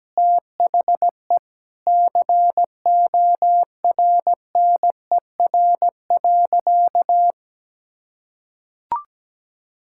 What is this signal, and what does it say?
Morse code, heard as a keyed tone, about 17 wpm, 700 Hz. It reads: THE CORNER.
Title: Sentences from Top 1600 Words 17wpm